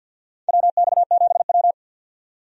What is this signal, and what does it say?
Morse code, heard as a keyed tone, about 53 wpm, 700 Hz. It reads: WX7W
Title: USA Call Signs - 4 Characters - 50wpm